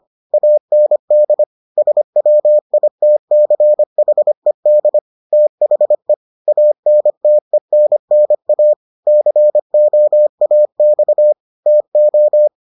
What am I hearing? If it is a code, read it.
Morse code, heard as a keyed tone, about 25 wpm, 600 Hz. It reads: AND SWITCHED THE ANTENNA COAX TO